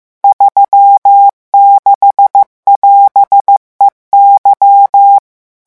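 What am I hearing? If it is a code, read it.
Morse code, heard as a keyed tone, about 15 wpm, 800 Hz. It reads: 36<AS>EY
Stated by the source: Morse revision lesson at 14 words per minute.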